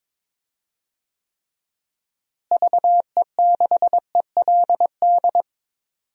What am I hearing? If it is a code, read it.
Morse code, heard as a keyed tone, about 21 wpm, 700 Hz. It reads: VE6ELD
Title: DXCC-100 Call Signs - 6 Characters - Repeated 3x - 22wpm